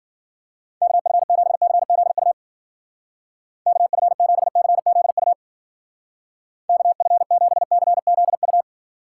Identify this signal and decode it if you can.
Morse code, heard as a keyed tone, about 51 wpm, 700 Hz. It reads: KF7CZU KF7CZU KF7CZU